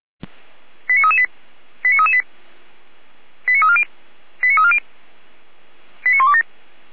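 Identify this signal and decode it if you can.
Five-tone selective call (ZVEI), heard as quick runs of five tones, tones of about 70 ms. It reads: 89338 89338 89340 89340 89127
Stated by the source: ZVEI five-tone selective call received by SDR.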